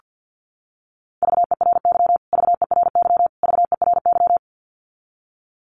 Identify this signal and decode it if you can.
Morse code, heard as a keyed tone, about 48 wpm, 700 Hz. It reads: VERY VERY VERY
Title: Top 300 Words Repeated 50wpm